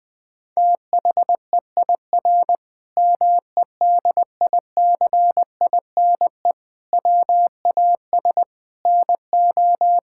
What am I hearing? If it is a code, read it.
Morse code, heard as a keyed tone, about 20 wpm, 700 Hz. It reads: THEIR MEDICINE WAS NO
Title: Sentences from Top 1000 Words - Review Part 6 - 20wpm